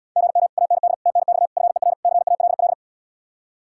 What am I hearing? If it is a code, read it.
Morse code, heard as a keyed tone, about 77 wpm, 700 Hz. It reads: GM AND TNX FER CALL